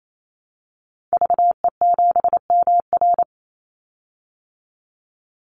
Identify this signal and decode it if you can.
Morse code, heard as a keyed tone, about 28 wpm, 700 Hz. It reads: VE7MR
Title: Call Signs Repeated 2x - 28wpm